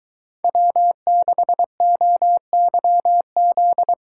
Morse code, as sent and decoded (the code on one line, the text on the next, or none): .-- -.... --- -.-- --..
W6OYZ